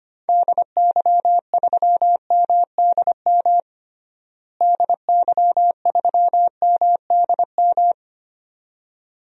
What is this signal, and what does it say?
Morse code, heard as a keyed tone, about 25 wpm, 700 Hz. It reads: DY3MDM DY3MDM
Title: DXCC-100 Call Signs - 6 Characters - Repeated 2x - 25wpm